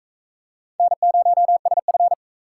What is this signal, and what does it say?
Morse code, heard as a keyed tone, about 39 wpm, 700 Hz. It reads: N0SF